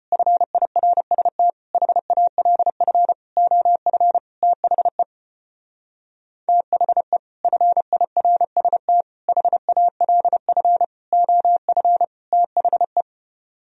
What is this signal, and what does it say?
Morse code, heard as a keyed tone, about 32 wpm, 700 Hz. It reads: FIRST HALF OF THE THE FIRST HALF OF THE